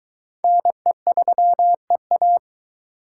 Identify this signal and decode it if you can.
Morse code, heard as a keyed tone, about 22 wpm, 700 Hz. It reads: NE3EA